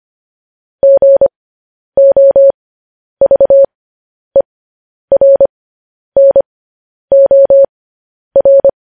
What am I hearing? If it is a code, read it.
Morse code, heard as a keyed tone, about 25 wpm, spaced as for 10 wpm, 550 Hz. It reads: GOVERNOR
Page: news-headlines-in-morse-code-at-10-wpm